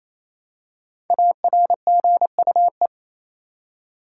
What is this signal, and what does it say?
Morse code, heard as a keyed tone, about 28 wpm, 700 Hz. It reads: ARGUE
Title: Five Letter Words - 28wpm